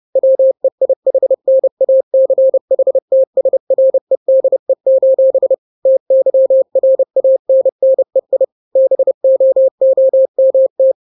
Morse code, heard as a keyed tone, about 30 wpm, 530 Hz. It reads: WEIHNACHTSREDE: TYRANNEI BOOMT
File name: 29wpm